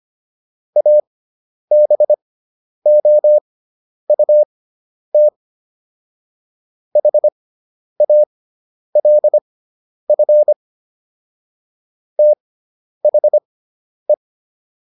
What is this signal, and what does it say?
Morse code, heard as a keyed tone, about 25 wpm, spaced as for 10 wpm, 600 Hz. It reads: ABOUT HALF THE